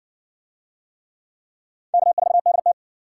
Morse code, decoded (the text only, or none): K4DT